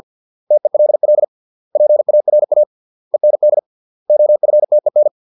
Morse code, heard as a keyed tone, about 50 wpm, 600 Hz. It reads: TELL PARA END OFTEN
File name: 632 words / 632 Words 50WPM